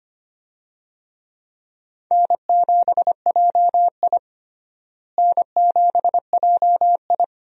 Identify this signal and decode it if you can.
Morse code, heard as a keyed tone, about 25 wpm, 700 Hz. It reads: N7JI N7JI
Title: USA Call Signs - 4 Characters - Repeated 3x - 25wpm